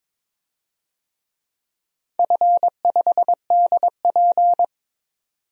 Morse code, heard as a keyed tone, about 22 wpm, 700 Hz. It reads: F5DP
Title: DXCC-100 Call Signs - 4 Characters - 22wpm